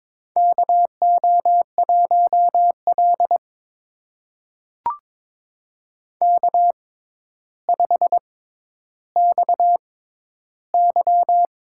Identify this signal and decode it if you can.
Morse code, heard as a keyed tone, about 22 wpm, spaced as for 11 wpm, 700 Hz. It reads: KO1L K5XY
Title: USA Call Signs - 4 Characters - 6x Character Spacing - 22wpm